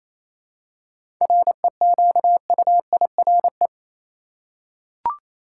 Morse code, heard as a keyed tone, about 28 wpm, 700 Hz. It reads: REQUIRE
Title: Seven Letter Words - 28wpm